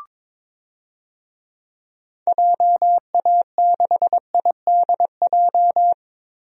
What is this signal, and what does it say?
Morse code, heard as a keyed tone, about 22 wpm, 700 Hz. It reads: JA6IDJ
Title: DXCC-100 Call Signs - 6 Characters - 22wpm